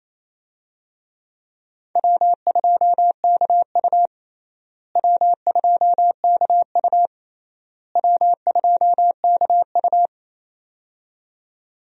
Morse code, heard as a keyed tone, about 28 wpm, 700 Hz. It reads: W2KU W2KU W2KU
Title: USA Call Signs - 4 Characters - Repeated 3x - 28wpm